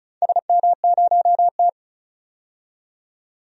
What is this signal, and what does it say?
Morse code, heard as a keyed tone, about 36 wpm, 700 Hz. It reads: SM0T
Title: DXCC-100 Call Signs - 4 Characters - 35wpm